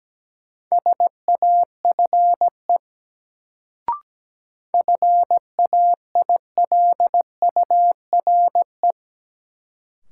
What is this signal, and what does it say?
Morse code, heard as a keyed tone, about 17 wpm, 700 Hz. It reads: SAFE FAILURE
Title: Sentences from Top 900 Words - New Word Drill - 17wpm